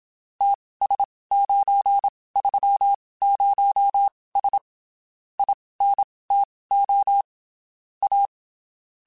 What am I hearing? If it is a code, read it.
Morse code, heard as a keyed tone, about 27 wpm, spaced as for 19 wpm, 800 Hz. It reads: TS930S INTO A
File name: (spaced) / QSOs Spaced-mp3s / kc7sac @25 sp